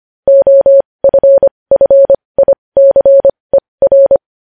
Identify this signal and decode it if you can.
Morse code, heard as a keyed tone, about 25 wpm, spaced as for 20 wpm, 550 Hz. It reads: OFFICER
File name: podcast / News headlines-1x-25x20